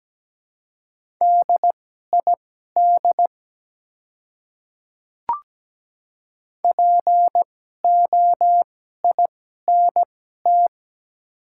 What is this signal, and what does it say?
Morse code, heard as a keyed tone, about 17 wpm, spaced as for 12 wpm, 700 Hz. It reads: DID POINT
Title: 2x Character Spacing - Top 200 Words - 17wpm